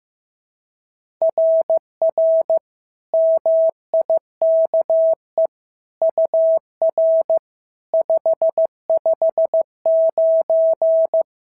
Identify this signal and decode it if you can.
Morse code, heard as a keyed tone, about 15 wpm, 650 Hz. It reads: RR MIKE UR 559
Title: SOTA QSO Practice 15wpm